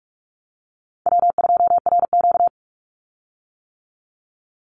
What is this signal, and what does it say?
Morse code, heard as a keyed tone, about 44 wpm, 700 Hz. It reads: W2RQ
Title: Call Signs - 45wpm